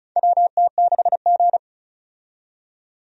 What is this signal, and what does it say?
Morse code, heard as a keyed tone, about 36 wpm, 700 Hz. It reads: WT6G
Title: USA Call Signs - 4 Characters - 35wpm